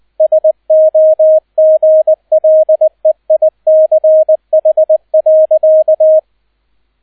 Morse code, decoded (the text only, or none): SOGLEICH.